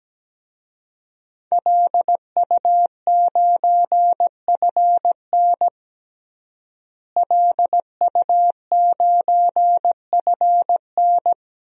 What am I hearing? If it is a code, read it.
Morse code, heard as a keyed tone, about 17 wpm, 700 Hz. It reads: LU9FN LU9FN
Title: DXCC-100 Call Signs - 5 Characters - Repeated 3x - 17wpm